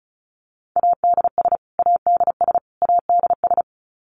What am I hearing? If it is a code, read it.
Morse code, heard as a keyed tone, about 34 wpm, 700 Hz. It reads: ADS ADS ADS
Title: Abbreviations Repeated 3x - 35wpm